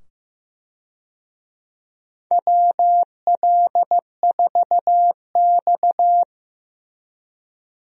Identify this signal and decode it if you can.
Morse code, heard as a keyed tone, about 15 wpm, 700 Hz. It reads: WL4X
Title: USA Call Signs - 4 Characters - 15wpm